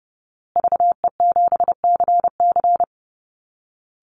Morse code, heard as a keyed tone, about 30 wpm, 700 Hz. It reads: VE7CC
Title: Call Signs Repeated 2x - 30wpm